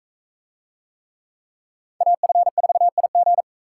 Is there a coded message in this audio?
Yes: Morse code from 2.0 to 3.4 s, heard as a keyed tone, about 42 wpm, 700 Hz. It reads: AF4IG